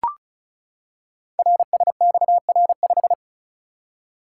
Morse code, heard as a keyed tone, about 35 wpm, 700 Hz. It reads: RSXR5
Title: Code Groups - 5 Letter:Numbers - 35wpm